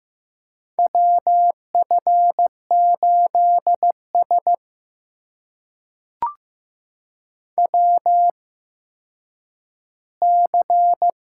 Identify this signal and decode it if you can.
Morse code, heard as a keyed tone, about 15 wpm, spaced as for 9 wpm, 700 Hz. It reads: WF8S WC
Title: USA Call Signs - 4 Characters - 8x Character Spacing - 15wpm